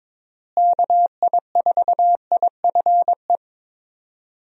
Morse code, heard as a keyed tone, about 22 wpm, 700 Hz. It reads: KI4IFE
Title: USA Call Signs - 6 Characters - Repeated 3x - 22wpm